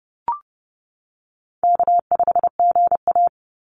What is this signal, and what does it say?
Morse code, heard as a keyed tone, about 30 wpm, 700 Hz. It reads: K5GA
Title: Easy Call Signs - 30wpm